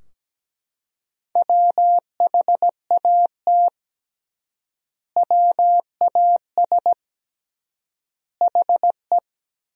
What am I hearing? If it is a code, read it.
Morse code, heard as a keyed tone, about 17 wpm, 700 Hz. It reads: WHAT WAS HE
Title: 3x Word-Spacing - Sentences from Top 100 Words - 17wpm